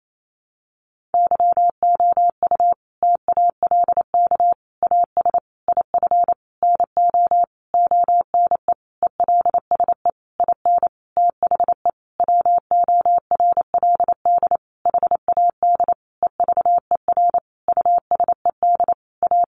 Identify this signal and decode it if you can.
Morse code, heard as a keyed tone, about 28 wpm, 700 Hz. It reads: YOU TALK AS IF NO ONE ELSE IN THE WORLD HAD EVER USED A
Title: Sentences from Top 1100 Words 28wpm